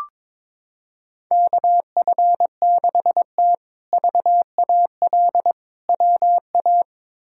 Morse code, heard as a keyed tone, about 22 wpm, 700 Hz. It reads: KF6T VAL WA
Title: K1USN SST QSO Practice - 22wpm